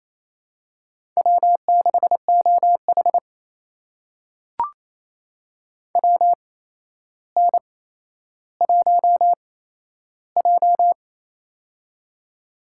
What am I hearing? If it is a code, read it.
Morse code, heard as a keyed tone, about 28 wpm, spaced as for 12 wpm, 700 Hz. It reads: W6OH WN1J
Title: USA Call Signs - 4 Characters - 8x Character Spacing - 28wpm